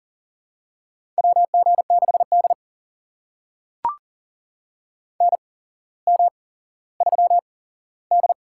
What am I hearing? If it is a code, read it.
Morse code, heard as a keyed tone, about 40 wpm, spaced as for 17 wpm, 700 Hz. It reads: WG6D NM3D
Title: USA Call Signs - 4 Characters - 8x Character Spacing - 40wpm